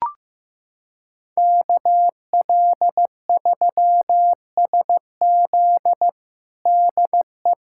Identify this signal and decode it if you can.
Morse code, heard as a keyed tone, about 15 wpm, 700 Hz. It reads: KL3SZ DE
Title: USA Callsign DE Callsign - 5 Characters - 15wpm